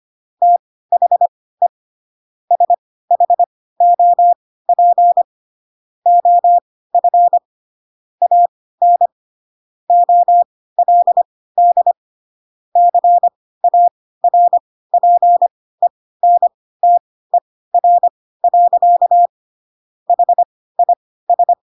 Morse code, heard as a keyed tone, about 25 wpm, spaced as for 16 wpm, 700 Hz. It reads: THE SHOP OF AN OLD CARPENTER. HIS
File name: Pinocchio / Pinocchio ch1 10 25 ik1hge